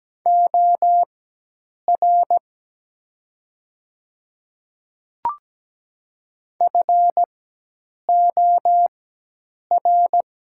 Morse code, heard as a keyed tone, about 17 wpm, spaced as for 8 wpm, 700 Hz. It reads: OR FOR
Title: 4x Character Spacing - Top 500 Words - 17wpm